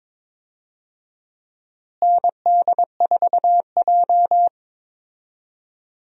Morse code, heard as a keyed tone, about 22 wpm, 700 Hz. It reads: ND4J
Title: USA Call Signs - 4 Characters - Repeated 3x - 22wpm